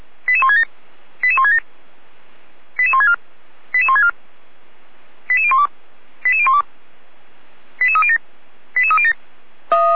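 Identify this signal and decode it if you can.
Five-tone selective call (ZVEI), heard as quick runs of five tones, tones of about 70 ms. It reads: 88167 88167 88164 88164 88012 88012 88307 88307
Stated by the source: ZVEI five-tone call 88199 received on SDR, Feuerwehr Orlamünde.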